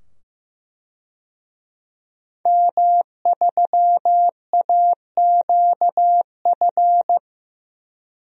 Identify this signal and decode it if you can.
Morse code, heard as a keyed tone, about 15 wpm, 700 Hz. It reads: M3AQF
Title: DXCC-100 Call Signs - 5 Characters - 15wpm